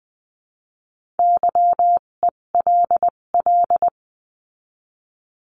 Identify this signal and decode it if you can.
Morse code, heard as a keyed tone, about 20 wpm, spaced as for 17 wpm, 700 Hz. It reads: YELL
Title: 257 Lesson 42b - Words 14-20wpm